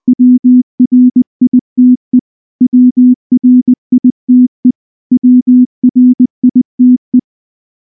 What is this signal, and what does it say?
Morse code, heard as a keyed tone, about 19 wpm, 250 Hz. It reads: WRITE WRITE WRITE
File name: morse / morse-training-20WPM-3